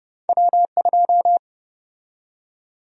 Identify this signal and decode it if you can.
Morse code, heard as a keyed tone, about 30 wpm, 700 Hz. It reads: W2
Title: USA Call Sign Prefixes - 30wpm